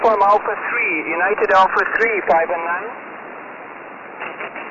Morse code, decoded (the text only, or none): none